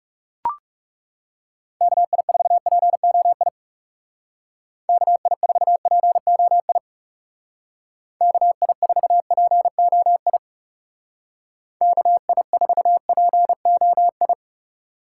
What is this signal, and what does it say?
Morse code, heard as a keyed tone, about 35 wpm, 700 Hz. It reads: KI4POI KI4POI KI4POI KI4POI
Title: Speed Racing - USA Call Signs 30wpm